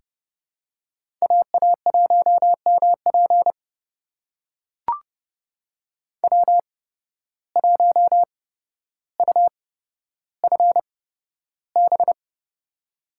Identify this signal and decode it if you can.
Morse code, heard as a keyed tone, about 31 wpm, spaced as for 13 wpm, 700 Hz. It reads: AA1MP W1UFB
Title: USA Call Signs - 5 Characters - 8x Character Spacing - 30wpm